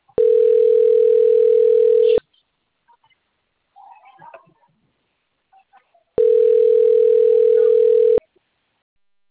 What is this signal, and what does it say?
Ringback tone (North America), 440 + 480 Hz, with speech.
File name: rg-1600-unknown- - - .48850